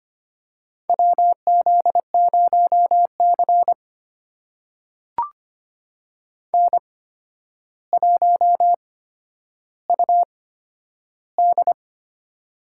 Morse code, heard as a keyed tone, about 25 wpm, spaced as for 11 wpm, 700 Hz. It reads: WZ0C N1UD
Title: USA Call Signs - 8x Character Spacing - 25wpm